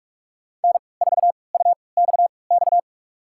Morse code, heard as a keyed tone, about 46 wpm, spaced as for 26 wpm, 700 Hz. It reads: N4UXX
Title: USA Call Signs - 5 Characters - 3x Character Spacing - 45wpm